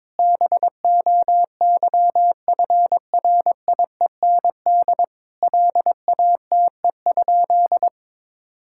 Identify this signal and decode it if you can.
Morse code, heard as a keyed tone, about 22 wpm, 700 Hz. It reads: BOYFRIEND LATE?
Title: Sentences from Top 1000 Words - Review Part 5 - 22wpm